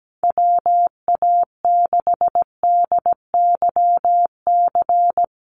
Morse code, heard as a keyed tone, about 17 wpm, 700 Hz. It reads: WA6DYC
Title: USA Call Signs Repeated 3x 17wpm